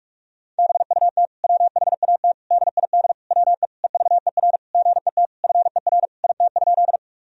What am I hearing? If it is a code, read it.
Morse code, heard as a keyed tone, about 41 wpm, 700 Hz. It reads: BUT WHAT DID WE EVER GET FER IT?
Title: Sentences from Top 400 Words and Abbreviations 45wpm